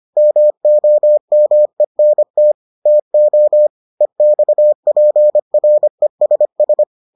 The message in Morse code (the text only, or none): MOMENT TO EXPRESS